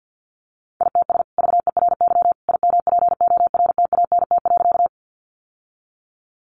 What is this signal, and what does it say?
Morse code, heard as a keyed tone, about 50 wpm, 700 Hz. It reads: ITS VERY IMPORTANT.
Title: Sentences from Top 600 Words 50wpm